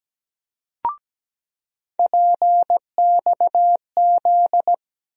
Morse code, heard as a keyed tone, about 17 wpm, 700 Hz. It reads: PXZ
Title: USA Call Sign Postfixes - 3 Letters 17wpm